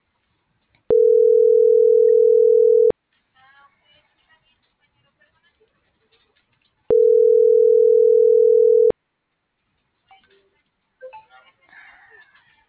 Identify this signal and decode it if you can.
Ringback tone (North America), 440 + 480 Hz.